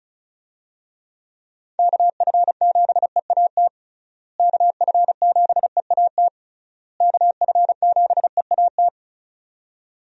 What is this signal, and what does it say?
Morse code, heard as a keyed tone, about 34 wpm, 700 Hz. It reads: KF7EAT KF7EAT KF7EAT
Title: USA Call Signs - 6 Characters - Repeated 3x - 35wpm